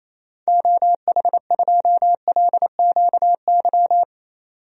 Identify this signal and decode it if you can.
Morse code, heard as a keyed tone, about 28 wpm, 700 Hz. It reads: OH2LQY
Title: DXCC-100 Call Signs - 6 Characters - 28wpm